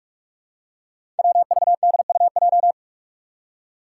Morse code, heard as a keyed tone, about 46 wpm, 700 Hz. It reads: WVDUJ
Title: Five Challenging Letter-Number Combinations 45wpm